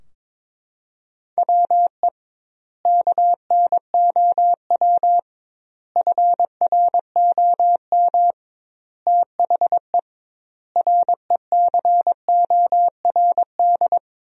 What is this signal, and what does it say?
Morse code, heard as a keyed tone, about 22 wpm, 700 Hz. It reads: WE KNOW FROM THE RECORD
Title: 2x Word-Spacing - Sentences from Top 500 Words - 22wpm